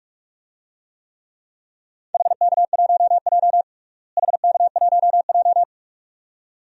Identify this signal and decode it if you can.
Morse code, heard as a keyed tone, about 46 wpm, 700 Hz. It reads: HK1J HK1J
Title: DXCC-100 Call Signs - 4 Characters - Repeated 2x - 45wpm